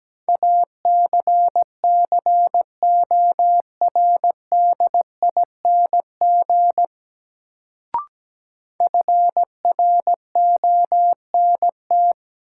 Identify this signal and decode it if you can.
Morse code, heard as a keyed tone, about 17 wpm, 700 Hz. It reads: ACCORDING FRONT
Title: Sets of 2 Words - New Word Drill - 17wpm